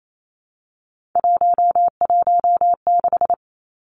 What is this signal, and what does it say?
Morse code, heard as a keyed tone, about 28 wpm, 700 Hz. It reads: 116
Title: Numbers - 3-Digit 28wpm